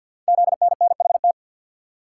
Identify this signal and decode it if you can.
Morse code, heard as a keyed tone, about 51 wpm, 700 Hz. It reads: ZNNHT
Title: Code Groups - 5 Letters - 50wpm